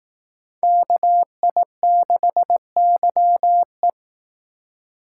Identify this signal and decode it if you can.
Morse code, heard as a keyed tone, about 18 wpm, 700 Hz. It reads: KI6YE
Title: Speed Racing - USA 2x2 Call Signs 15wpm